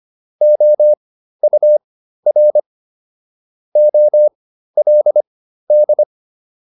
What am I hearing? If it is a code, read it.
Morse code, heard as a keyed tone, about 25 wpm, spaced as for 13 wpm, 600 Hz. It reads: OUR OLD
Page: intermediate-practice-files